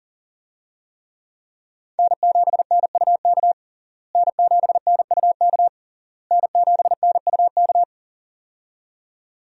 Morse code, heard as a keyed tone, about 40 wpm, 700 Hz. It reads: N7NUK N7NUK N7NUK
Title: USA Call Signs - 5 Characters - Repeated 3x - 40wpm